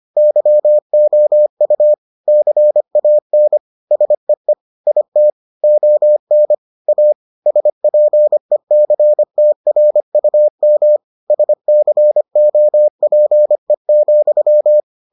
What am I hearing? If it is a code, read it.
Morse code, heard as a keyed tone, about 25 wpm, 600 Hz. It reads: YOU CAN SEE IT ON A SPECTRUM SCOPE,